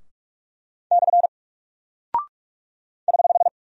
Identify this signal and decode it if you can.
Morse code, heard as a keyed tone, about 44 wpm, 700 Hz. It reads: / <HH>
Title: Single Character (Letter:Number:Punctuation:Prosign) Alt 45wpm